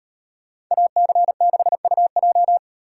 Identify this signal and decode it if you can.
Morse code, heard as a keyed tone, about 35 wpm, 700 Hz. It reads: AC6UJ